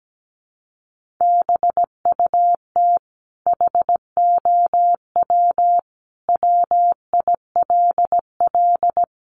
Morse code, heard as a keyed tone, about 17 wpm, 700 Hz. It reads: BUT HOW WILL
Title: Sentences from Top 1400 Words 17wpm